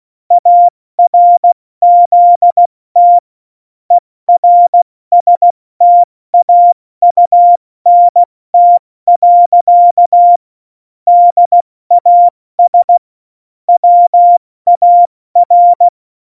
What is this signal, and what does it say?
Morse code, heard as a keyed tone, about 16 wpm, spaced as for 14 wpm, 700 Hz. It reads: ARZT ERSTAUNT. DAS WAR
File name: Morsekurs / Witze / Witz 15. September 2020, 70 F80